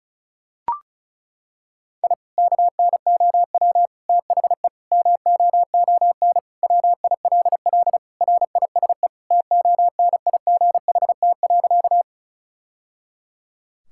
Morse code, heard as a keyed tone, about 36 wpm, 700 Hz. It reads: I KNOW THE MOON WILL RISE TONIGHT.
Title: Sentences from Top 1000 Words - Review Part 6 - 35wpm